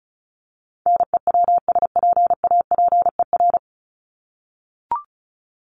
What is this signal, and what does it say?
Morse code, heard as a keyed tone, about 34 wpm, 700 Hz. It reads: NEWSPAPER
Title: Top 751-1000 Words 35wpm